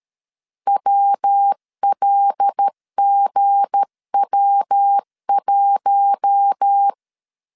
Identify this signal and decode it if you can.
Morse code, heard as a keyed tone, about 13 wpm, 800 Hz. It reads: WLGW1
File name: LEZIONI / 7 GF9Z / 7 GF9Z 12wpm 2147